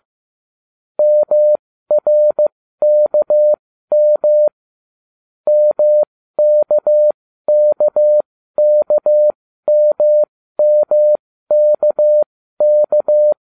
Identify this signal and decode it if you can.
Morse code, heard as a keyed tone, about 15 wpm, spaced as for 13 wpm, 600 Hz. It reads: MRKM MKKKMMKK